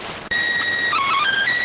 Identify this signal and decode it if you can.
SSTV (slow-scan television), PD 290.